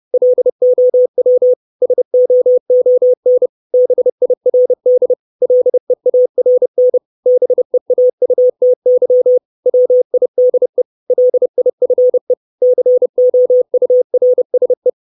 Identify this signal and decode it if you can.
Morse code, heard as a keyed tone, about 30 wpm, 500 Hz. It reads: LOW SOON BIRD LEARN BEAUTY WIDE LIFE COURSE
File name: 632 Words 30WPM